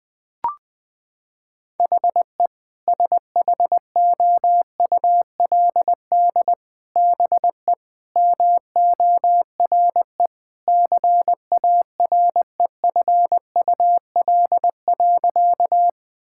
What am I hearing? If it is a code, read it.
Morse code, heard as a keyed tone, about 20 wpm, 700 Hz. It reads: HE SHOULD BE MORE CAREFUL.
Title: Sentences from Top 1600 Words 20wpm